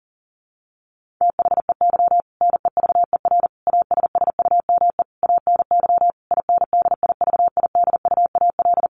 Morse code, heard as a keyed tone, about 40 wpm, 700 Hz. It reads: THEY NEVER ASSUME ANY INDIVIDUAL
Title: Sentences from Top 1500 Words 40wpm